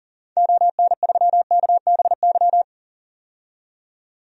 Morse code, heard as a keyed tone, about 40 wpm, 700 Hz. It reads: ON3KBY
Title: DXCC-100 Call Signs - 6 Characters - 40wpm